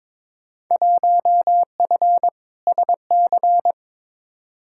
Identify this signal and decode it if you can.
Morse code, heard as a keyed tone, about 21 wpm, 700 Hz. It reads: 1F SC
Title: Field Day Exchanges Encore - 22wpm